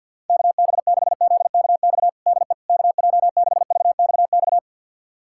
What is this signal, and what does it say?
Morse code, heard as a keyed tone, about 51 wpm, 700 Hz. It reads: KB6ZKX DE KJ6VXX
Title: USA Callsign DE Callsign - 6 Characters - 50wpm